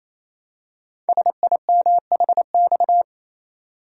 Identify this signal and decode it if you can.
Morse code, heard as a keyed tone, about 29 wpm, 700 Hz. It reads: SIMHX
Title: Code Groups - 5 Letters - 28wpm